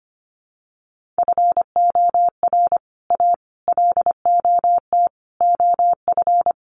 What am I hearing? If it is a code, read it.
Morse code, heard as a keyed tone, about 25 wpm, 700 Hz. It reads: FOR A LOT OF
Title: Sets of 5 Words Encore 25wpm